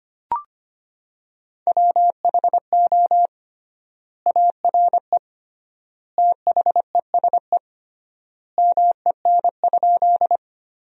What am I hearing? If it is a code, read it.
Morse code, heard as a keyed tone, about 25 wpm, 700 Hz. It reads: WHO ARE THESE MEN?
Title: 3x Word-Spacing - Sentences from Top 200 Words - 25wpm